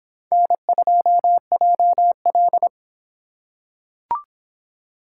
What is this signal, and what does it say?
Morse code, heard as a keyed tone, about 23 wpm, 700 Hz. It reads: N2JL